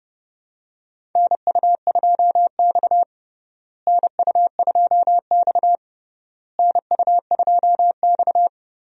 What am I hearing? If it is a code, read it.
Morse code, heard as a keyed tone, about 32 wpm, 700 Hz. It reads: NU2X NU2X NU2X
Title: USA Call Signs - 4 Characters - Repeated 3x - 30wpm